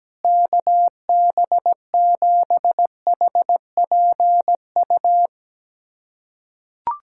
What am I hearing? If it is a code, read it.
Morse code, heard as a keyed tone, about 17 wpm, 700 Hz. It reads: KB7HPU
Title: USA Callsign DE Callsign - 6 Characters - 17wpm